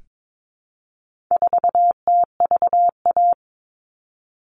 Morse code, heard as a keyed tone, about 22 wpm, 700 Hz. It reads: 4TVA
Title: All Cut Numbers - 4-Digit 22wpm